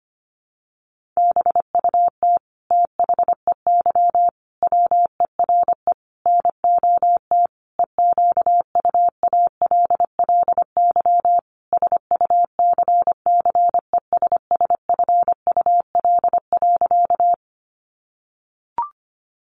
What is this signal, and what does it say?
Morse code, heard as a keyed tone, about 25 wpm, 700 Hz. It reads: BUT THEY WERE NOT EQUALLY SUCCESSFUL.
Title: Sentences from Top 1400 Words 25wpm